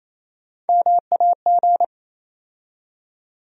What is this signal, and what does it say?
Morse code, heard as a keyed tone, about 28 wpm, 700 Hz. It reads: MAG
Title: USA Call Sign Postfixes - 3 Letters 28wpm